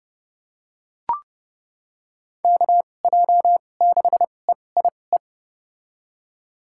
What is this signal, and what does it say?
Morse code, heard as a keyed tone, about 31 wpm, spaced as for 22 wpm, 700 Hz. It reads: KJ6EIE
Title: USA Call Signs - 2x Character Spacing - 30wpm